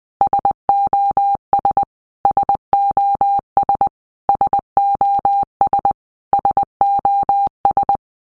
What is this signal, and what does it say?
Morse code, heard as a keyed tone, about 20 wpm, 800 Hz. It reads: SOS SOS SOS SOS
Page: SOS sound in Morse Code (Medium)